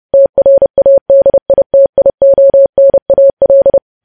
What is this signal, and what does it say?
Morse code, heard as a keyed tone, about 30 wpm, 550 Hz. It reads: TRADITIONAL
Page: News Headlines in Morse Code at 30 WPM